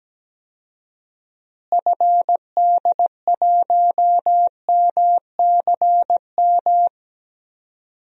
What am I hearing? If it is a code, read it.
Morse code, heard as a keyed tone, about 17 wpm, 700 Hz. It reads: FD1MCM
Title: DXCC-100 Call Signs - 6 Characters - Repeated 3x - 17wpm